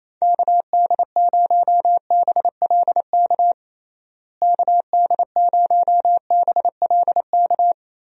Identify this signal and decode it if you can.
Morse code, heard as a keyed tone, about 28 wpm, 700 Hz. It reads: KD0BLK KD0BLK
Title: USA Call Signs - 6 Characters - Repeated 3x - 28wpm